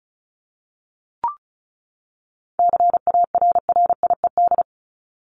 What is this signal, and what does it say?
Morse code, heard as a keyed tone, about 34 wpm, 700 Hz. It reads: CARRIED
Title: ED Suffix - Top 100 Words with Base Words 35wpm